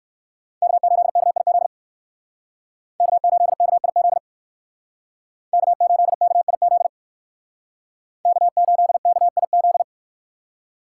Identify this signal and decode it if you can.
Morse code, heard as a keyed tone, about 56 wpm, 700 Hz. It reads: K8KIZ K8KIZ K8KIZ K8KIZ
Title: Speed Racing - USA Call Signs 45wpm